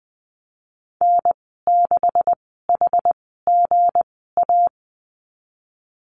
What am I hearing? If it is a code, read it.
Morse code, heard as a keyed tone, about 20 wpm, spaced as for 15 wpm, 700 Hz. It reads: N6HGA
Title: 173 Lesson 28g - Number 9 - Call Signs 20wpm